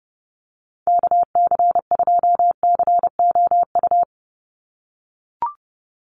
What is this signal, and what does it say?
Morse code, heard as a keyed tone, about 28 wpm, 700 Hz. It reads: KC2COU
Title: USA Call Signs 30wpm